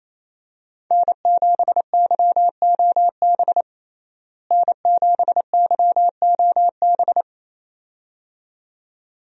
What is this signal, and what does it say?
Morse code, heard as a keyed tone, about 28 wpm, 700 Hz. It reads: N7YOB N7YOB
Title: USA Call Signs - 5 Characters - Repeated 2x - 28wpm